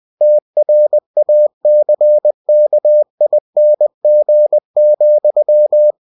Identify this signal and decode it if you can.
Morse code, heard as a keyed tone, about 20 wpm, 600 Hz. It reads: TRACKING,
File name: cw-books / winnie-the-pooh / winnie-20wpm-page-0022